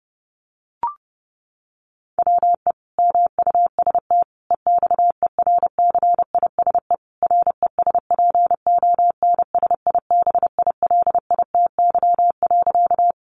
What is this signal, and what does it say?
Morse code, heard as a keyed tone, about 30 wpm, 700 Hz. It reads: WE MUST EXERCISE RESPONSIBILITY.
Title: Sentences from Top 1200 Words 30wpm